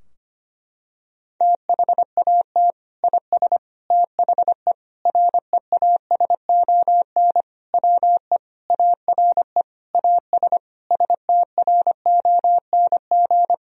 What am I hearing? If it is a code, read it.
Morse code, heard as a keyed tone, about 25 wpm, 690 Hz. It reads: THAT IS THE REASON WE ARE AS STRONG
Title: Sentences from Top 1000 Words - Review Part 4 - 25wpm